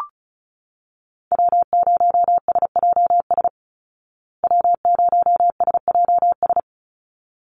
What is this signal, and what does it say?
Morse code, heard as a keyed tone, about 34 wpm, 700 Hz. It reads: W0SJS W0SJS
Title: Call Signs Repeated 3x - 35wpm